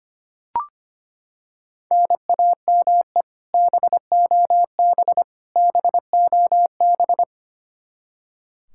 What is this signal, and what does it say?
Morse code, heard as a keyed tone, about 25 wpm, 700 Hz. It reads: NAME BOB BOB
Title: K1USN SST QSO Practice 25wpm